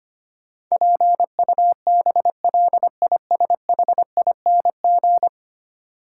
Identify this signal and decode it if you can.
Morse code, heard as a keyed tone, about 25 wpm, 700 Hz. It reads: PUBLISHING
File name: 10 Letter Words - First 3 Letters Given - 25wpm